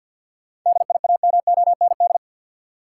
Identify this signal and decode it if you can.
Morse code, heard as a keyed tone, about 50 wpm, 700 Hz. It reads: DIAMOND
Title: Seven Letter Words - 50wpm